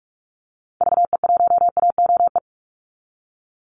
Morse code, heard as a keyed tone, about 44 wpm, 700 Hz. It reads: VE1AOE
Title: Call Signs Repeated 3x - 45wpm